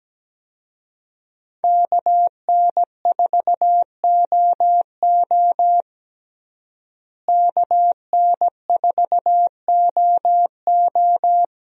Morse code, heard as a keyed tone, about 17 wpm, 700 Hz. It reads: KN4OO KN4OO
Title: USA Call Signs - 5 Characters - Repeated 3x - 17wpm